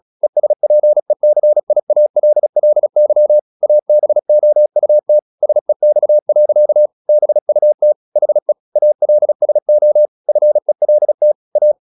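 Morse code, heard as a keyed tone, about 36 wpm, 600 Hz. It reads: ESPECIALLY ABOUT SEX. BUT HE ALSO FELT A
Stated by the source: Morse practice at 36 wpm, text of Ken Follett's novel Never, from CW med Gnister.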